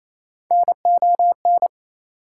Morse code, heard as a keyed tone, about 28 wpm, 700 Hz. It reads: NON
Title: Prefixes and Suffixes 28wpm